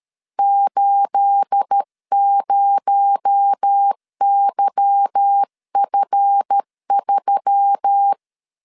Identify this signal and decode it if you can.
Morse code, heard as a keyed tone, about 13 wpm, 800 Hz. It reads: 80YF3
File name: LEZIONI / 9 28 12wpm 2351